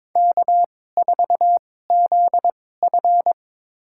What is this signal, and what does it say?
Morse code, heard as a keyed tone, about 22 wpm, spaced as for 16 wpm, 700 Hz. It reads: K4ZF
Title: USA Call Signs - 2x Character Spacing - 22wpm